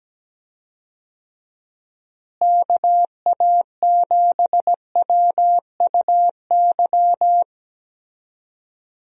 Morse code, heard as a keyed tone, about 17 wpm, 700 Hz. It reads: KA7WUY
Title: USA Call Signs - 6 Characters - Repeated 2x - 17wpm